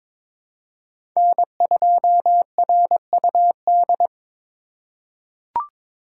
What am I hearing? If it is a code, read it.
Morse code, heard as a keyed tone, about 22 wpm, 700 Hz. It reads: N2RUD
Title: USA Call Signs - 4x Character Spacing - 22wpm